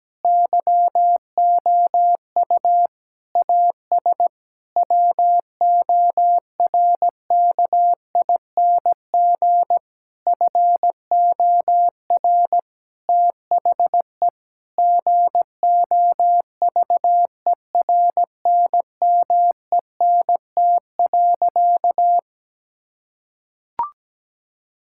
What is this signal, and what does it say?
Morse code, heard as a keyed tone, about 17 wpm, 700 Hz. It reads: YOU AS WORKING FOR THE GOVERNMENT.
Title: Sentences from Top 1000 Words - Review Part 6 - 17wpm